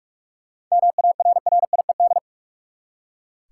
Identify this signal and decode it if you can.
Morse code, heard as a keyed tone, about 41 wpm, 700 Hz. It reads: MARRIED
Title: Seven Letter Words - 45wpm